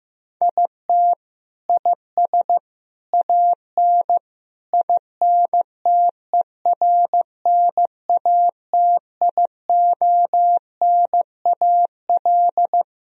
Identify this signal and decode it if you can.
Morse code, heard as a keyed tone, about 15 wpm, 700 Hz. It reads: IT IS AN INTERNATIONAL
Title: Sentences from Top 1000 Words - Review Part 4 - 15wpm